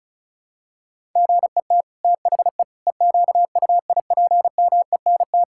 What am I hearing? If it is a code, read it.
Morse code, heard as a keyed tone, about 36 wpm, 700 Hz. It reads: GET THE EQUIPMENT
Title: Sentences from Top 1000 Words - Review Part 5 - 35wpm